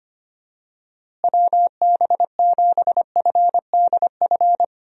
Morse code, heard as a keyed tone, about 25 wpm, 700 Hz. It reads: WB7FDF
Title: USA Call Signs - 6 Characters - Repeated 2x - 25wpm